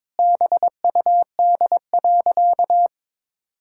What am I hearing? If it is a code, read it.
Morse code, heard as a keyed tone, about 22 wpm, 700 Hz. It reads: BUD.
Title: Pangram Challenge v2 - 22wpm